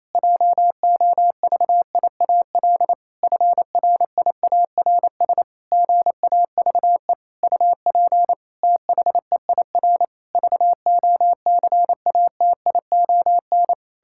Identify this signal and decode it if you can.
Morse code, heard as a keyed tone, about 28 wpm, 700 Hz. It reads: JOVIAL FRIARS GAVE UP THEIR VOCATION
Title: Pangram Challenge v2 - 28wpm